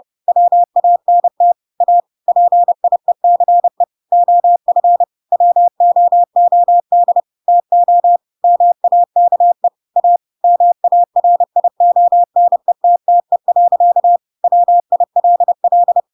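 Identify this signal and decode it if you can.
Morse code, heard as a keyed tone, about 30 wpm, 700 Hz. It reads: WANT A PIECE OF WOOD TO MAKE A MARIONETTE. WILL
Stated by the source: Chapter 2 of The Adventures of Pinocchio read as a book in Morse code (CW) at 30 wpm, by ik1hge.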